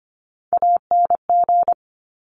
Morse code, heard as a keyed tone, about 25 wpm, 700 Hz. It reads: ANG
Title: Top Three Letter Combinations in Words 25wpm